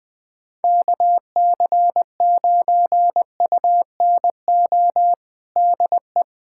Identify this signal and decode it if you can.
Morse code, heard as a keyed tone, about 20 wpm, 700 Hz. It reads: KC9UNO DE
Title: USA Callsign DE Callsign 20wpm